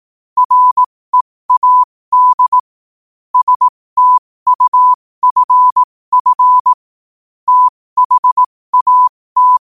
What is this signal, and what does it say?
Morse code, heard as a keyed tone, about 18 wpm, spaced as for 15 wpm, 1000 Hz. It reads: READ STUFF THAT
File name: qotdfiles / 15wpm